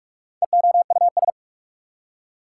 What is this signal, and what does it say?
Morse code, heard as a keyed tone, about 61 wpm, spaced as for 53 wpm, 700 Hz. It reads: EOUS
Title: Speed-Racing Prefixes and Suffixes 45wpm